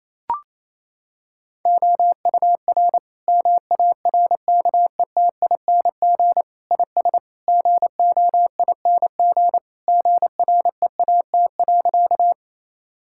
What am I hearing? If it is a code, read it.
Morse code, heard as a keyed tone, about 28 wpm, 700 Hz. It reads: OUR MARKETING IS GOING GREAT.
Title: Sentences from Top 1000 Words - Review Part 5 - 28wpm